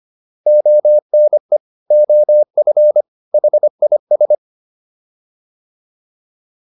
Morse code, heard as a keyed tone, about 25 wpm, 600 Hz. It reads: ONE OF HIS